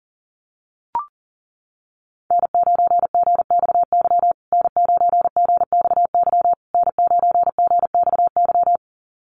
Morse code, heard as a keyed tone, about 40 wpm, 700 Hz. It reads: N9GXY N9GXY N9GXY
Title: USA Call Signs Repeated 3x 40wpm